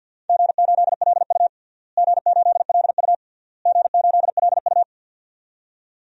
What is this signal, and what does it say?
Morse code, heard as a keyed tone, about 53 wpm, 700 Hz. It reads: G8LU G8LU G8LU
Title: DXCC-100 Call Signs - 4 Characters - Repeated 3x - 50wpm